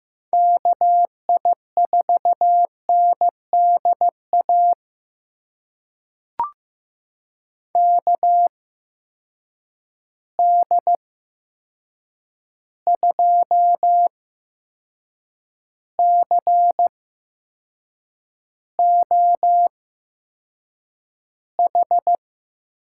Morse code, heard as a keyed tone, about 15 wpm, spaced as for 6 wpm, 700 Hz. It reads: KI4NDA KD2COH
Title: USA Call Signs - 6 Characters - 8x Character Spacing - 15wpm